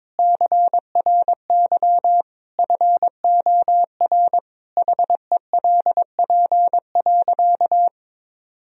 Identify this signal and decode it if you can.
Morse code, heard as a keyed tone, about 22 wpm, 700 Hz. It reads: CRY FOR HELP.
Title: Sentences from Top 1000 Words - Review Part 2 - 22wpm